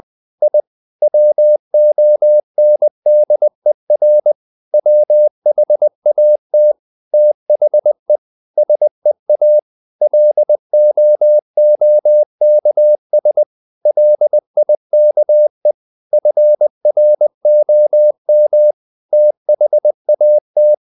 Morse code, heard as a keyed tone, about 20 wpm, 600 Hz. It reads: I WONDER WHAT THE SEA LOOKS LIKE FROM THAT